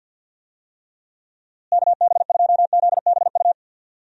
Morse code, heard as a keyed tone, about 52 wpm, 700 Hz. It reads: KB2ZBU
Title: USA Call Signs - 6 Characters - Repeated 3x - 50wpm